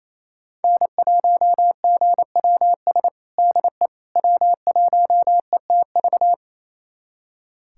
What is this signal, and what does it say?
Morse code, heard as a keyed tone, about 28 wpm, 700 Hz. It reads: N1GWS DE W1ETV
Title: USA Callsign DE Callsign - 5 Characters - 28wpm